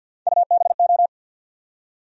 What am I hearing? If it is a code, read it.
Morse code, heard as a keyed tone, about 53 wpm, 700 Hz. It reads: UBO
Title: USA Call Sign Postfixes - 3 Letters 50wpm